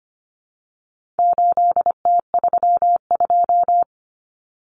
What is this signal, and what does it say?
Morse code, heard as a keyed tone, about 25 wpm, 700 Hz. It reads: 8T32
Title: Common Cut Numbers - 4-Digit 25wpm